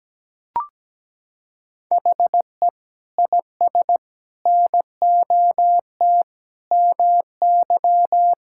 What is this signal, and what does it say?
Morse code, heard as a keyed tone, about 17 wpm, 700 Hz. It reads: HE IS NOT MY
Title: Sentences from Top 1600 Words 17wpm